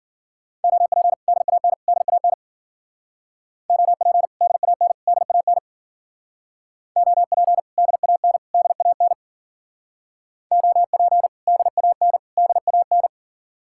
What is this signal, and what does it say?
Morse code, heard as a keyed tone, about 49 wpm, 700 Hz. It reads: OP DAN DAN OP DAN DAN OP DAN DAN OP DAN DAN
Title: Speed-Racing Common QSO Phrases - 40wpm